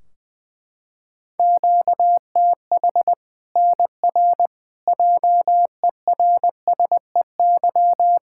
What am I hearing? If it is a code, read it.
Morse code, heard as a keyed tone, about 21 wpm, 700 Hz. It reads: QTH NR JERSEY
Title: K1USN SST QSO Practice 20wpm